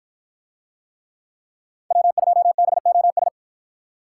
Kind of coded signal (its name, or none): Morse code